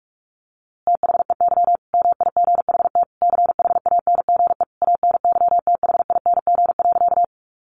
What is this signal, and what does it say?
Morse code, heard as a keyed tone, about 46 wpm, 700 Hz. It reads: THEY MIGHT CHANGE ANYTHING.
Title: A Sound Of Thunder 45wpm